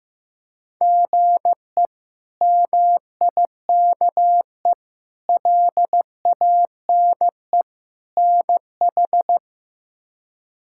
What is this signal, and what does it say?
Morse code, heard as a keyed tone, about 15 wpm, 700 Hz. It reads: GE MIKE LANE NH
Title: K1USN SST QSO Practice - 15wpm